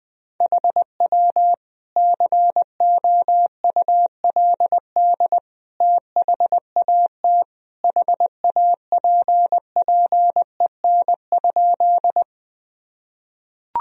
Morse code, heard as a keyed tone, about 20 wpm, 700 Hz. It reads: HW COULD THAT HAPPEN?
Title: Sentences from Top 400 Words and Abbreviations 20wpm